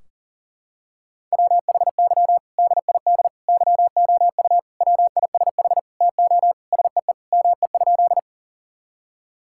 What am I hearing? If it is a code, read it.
Morse code, heard as a keyed tone, about 40 wpm, 700 Hz. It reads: WHY DID YOU WISH TO SEE ME?
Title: Sentences from Top 1600 Words 40wpm